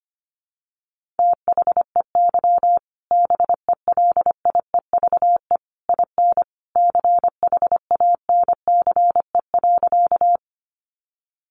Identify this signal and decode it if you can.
Morse code, heard as a keyed tone, about 25 wpm, 700 Hz. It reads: THEY BELIEVE IN CHANCE.
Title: Sentences from Top 1000 Words 25wpm